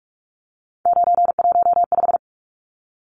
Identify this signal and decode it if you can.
Morse code, heard as a keyed tone, about 44 wpm, 700 Hz. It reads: 915
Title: Numbers - 3-Digit 45wpm